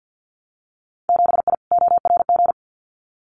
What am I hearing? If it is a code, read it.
Morse code, heard as a keyed tone, about 50 wpm, 700 Hz. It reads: 7I ORG
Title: Winter Field Day Exchanges Encore 50wpm